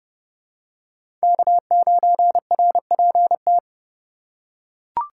Morse code, heard as a keyed tone, about 30 wpm, 700 Hz. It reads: K9RPT
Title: USA Call Signs - 5 Characters - 2x Character Spacing - 30wpm